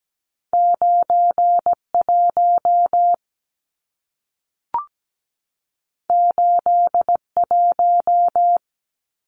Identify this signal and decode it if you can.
Morse code, heard as a keyed tone, about 17 wpm, 700 Hz. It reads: 91 81
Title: Numbers - 2-Digit 17wpm